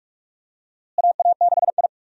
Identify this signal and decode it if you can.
Morse code, heard as a keyed tone, about 46 wpm, 700 Hz. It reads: AA6I
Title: USA Call Signs - 4 Characters - Repeated 3x - 45wpm